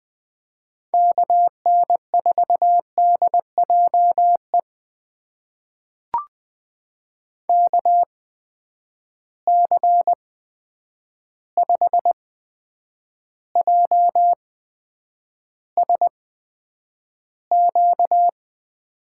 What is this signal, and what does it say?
Morse code, heard as a keyed tone, about 20 wpm, spaced as for 9 wpm, 700 Hz. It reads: KN4DJE KC5JSQ
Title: USA Call Signs - 6 Characters - 8x Character Spacing - 20wpm